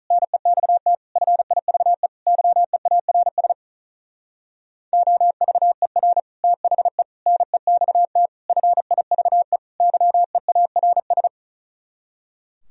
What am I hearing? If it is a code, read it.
Morse code, heard as a keyed tone, about 46 wpm, 700 Hz. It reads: NEXT FIVE YEARS OVER THE NEXT FIVE YEARS